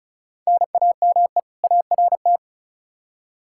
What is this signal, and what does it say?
Morse code, heard as a keyed tone, about 36 wpm, 700 Hz. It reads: NAME ART
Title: K1USN SST QSO Practice 35wpm